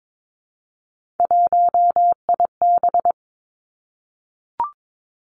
Morse code, heard as a keyed tone, about 22 wpm, 700 Hz. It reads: 1IB
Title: Three Challenging Letter-Number Combinations 22wpm